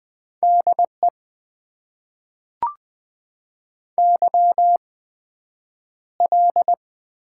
Morse code, heard as a keyed tone, about 20 wpm, spaced as for 9 wpm, 700 Hz. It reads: DE YL
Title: Top 90 QSO Elements - 8x Character Spacing - 20wpm